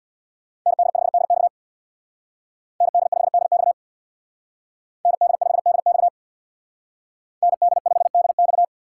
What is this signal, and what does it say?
Morse code, heard as a keyed tone, about 64 wpm, 700 Hz. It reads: ND5DX ND5DX ND5DX ND5DX